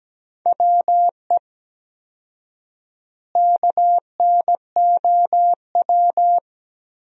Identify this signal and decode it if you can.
Morse code, heard as a keyed tone, about 17 wpm, 700 Hz. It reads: WE KNOW
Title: 4x Word-Spacing - Sentences from Top 500 Words - 17wpm